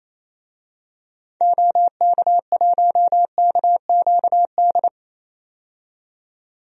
Morse code, heard as a keyed tone, about 28 wpm, 700 Hz. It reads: OK1KQD
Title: DXCC-100 Call Signs - 6 Characters - 28wpm